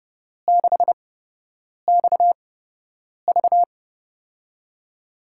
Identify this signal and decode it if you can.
Morse code, heard as a keyed tone, about 32 wpm, spaced as for 8 wpm, 700 Hz. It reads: 6XV
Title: USA Call Signs - 8x Character Spacing - 30wpm